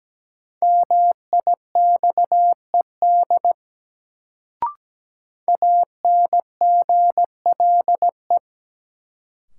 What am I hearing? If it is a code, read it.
Morse code, heard as a keyed tone, about 17 wpm, 700 Hz. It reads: MIXED ANGLE
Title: Sentences from Top 1500 Words - New Word Drill Alt - 17wpm